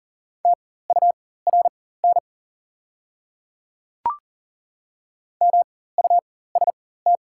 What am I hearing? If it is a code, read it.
Morse code, heard as a keyed tone, about 40 wpm, spaced as for 19 wpm, 700 Hz. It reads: TURN MUST
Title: 4x Character Spacing - Top 200 Words - 40wpm